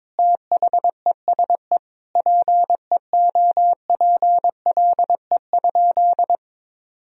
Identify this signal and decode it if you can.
Morse code, heard as a keyed tone, about 21 wpm, 700 Hz. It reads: THESE PEOPLE?
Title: Sentences from Top 1000 Words - Review Part 3 - 22wpm